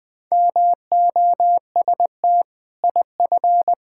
Morse code, heard as a keyed tone, about 20 wpm, 700 Hz. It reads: MOST IF
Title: Sentences with 6-Word Phrases - Spoken Beginning - 20wpm